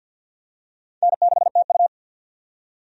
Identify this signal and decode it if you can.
Morse code, heard as a keyed tone, about 50 wpm, 700 Hz. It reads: N6TU
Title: USA Call Signs - 4 Characters - 50wpm